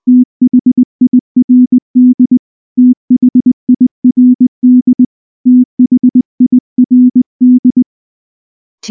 Morse code, heard as a keyed tone, about 20 wpm, 250 Hz. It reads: THIRD THIRD THIRD
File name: morse / morse-training-20WPM-2